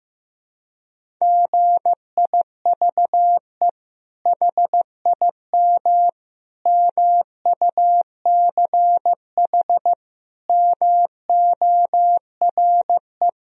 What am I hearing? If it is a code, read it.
Morse code, heard as a keyed tone, about 16 wpm, 700 Hz. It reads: GIVE HIM MUCH MORE
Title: Sentences from Top 1000 Words - Review Part 1 - 15wpm